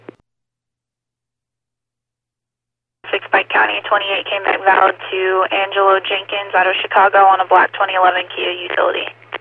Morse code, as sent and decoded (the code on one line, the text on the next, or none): none